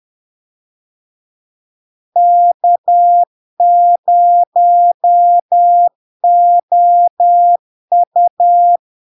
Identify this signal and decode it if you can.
Morse code, heard as a keyed tone, about 10 wpm, 700 Hz. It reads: K0OU
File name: LARK-FD-2022-Log-in-CW-10wpm-700-Hz